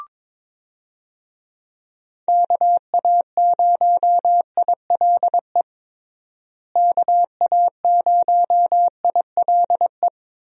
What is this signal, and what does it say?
Morse code, heard as a keyed tone, about 22 wpm, 700 Hz. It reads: KA0ILE KA0ILE
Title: USA Call Signs - 6 Characters - Repeated 2x - 22wpm